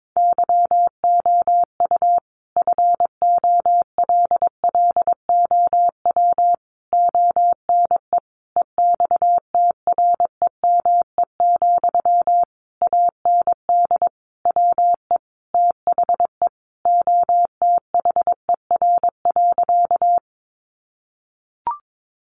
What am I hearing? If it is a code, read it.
Morse code, heard as a keyed tone, about 22 wpm, 700 Hz. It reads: YOU FOLLOW ONE EXTREME, AND WE THE OTHER.
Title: Sentences from Top 1300 Words 22wpm